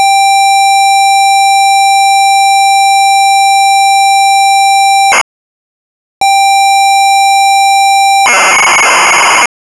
ZX Spectrum tape loading.
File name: Sinclair Zx Spectrum / Robin of the Wood (Unknown)